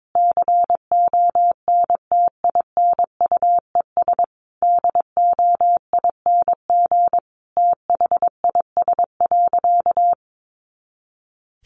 Morse code, heard as a keyed tone, about 22 wpm, 700 Hz. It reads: CONTINUES DOING THIS.
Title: Easy Phrases 22wpm